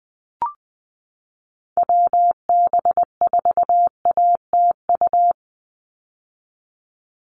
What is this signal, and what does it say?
Morse code, heard as a keyed tone, about 20 wpm, 700 Hz. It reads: WB4ATU
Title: USA Call Signs 20wpm